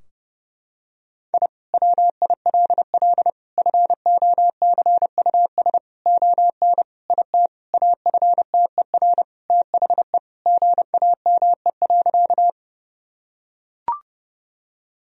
Morse code, heard as a keyed tone, about 30 wpm, 700 Hz. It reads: I WILL FOCUS ON IT AFTER THE GAME.
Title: Sentences from Top 1000 Words - Review Part 4 - 30wpm